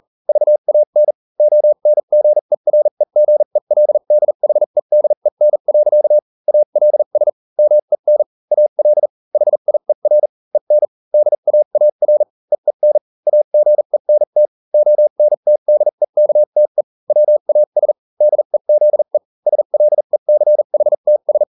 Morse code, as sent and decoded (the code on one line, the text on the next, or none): ...- .- -. / --- -. --. .-. --. .-.. -.. .... -.. -. .-.-.- / .- .-.. ... / -- . -. / .- .-.. / .... .. .-. / . -. / -.. .- .- .-. / . -. / .- --. -. - / --- -. - -.. -.- - . / .-- .- ... / -.. --.. / ... .-.. -.-. .... - ...
VAN ONGEREGELDHEDEN. ALS MEN AL HIER EN DAAR EEN AGENT ONTDEKTE WAS DEZE SLECHTS